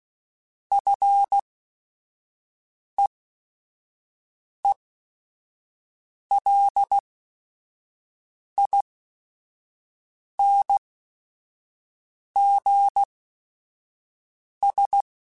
Morse code, heard as a keyed tone, about 16 wpm, spaced as for 5 wpm, 800 Hz. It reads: FEELINGS
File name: Quotes / Quotes-mp3s / ChildAsleep @5